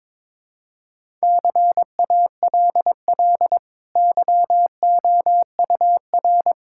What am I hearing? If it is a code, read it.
Morse code, heard as a keyed tone, about 22 wpm, 700 Hz. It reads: CALL YOUR
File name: practice / 22wpm / Easy Sentences with Hints - 22wpm